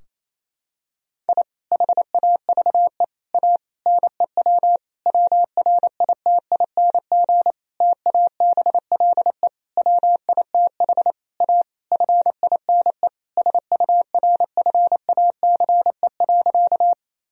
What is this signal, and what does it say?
Morse code, heard as a keyed tone, about 27 wpm, 700 Hz. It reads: I HAVE A NEW WRITING TABLE WITH A FINE SURFACE.
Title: Sentences from Top 1000 Words - Review Part 3 - 28wpm